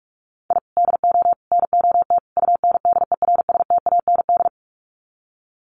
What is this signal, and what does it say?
Morse code, heard as a keyed tone, about 44 wpm, 700 Hz. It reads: I DO NOT UNDERSTAND
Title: Sets of 4 Words 45wpm